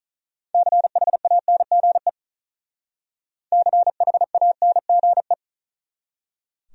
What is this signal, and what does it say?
Morse code, heard as a keyed tone, about 44 wpm, 700 Hz. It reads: CHANGE CHANGE
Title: Speed-Racing Top 200 Words 35wpm